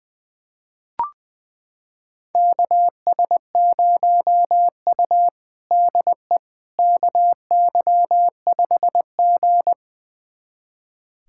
Morse code, heard as a keyed tone, about 21 wpm, 700 Hz. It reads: KS0U DE KY5G
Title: USA Callsign DE Callsign - 4 Characters - 20wpm